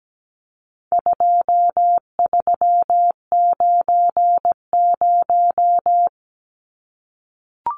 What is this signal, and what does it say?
Morse code, heard as a keyed tone, about 17 wpm, 700 Hz. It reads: 2390
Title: Numbers - 4-Digit 17wpm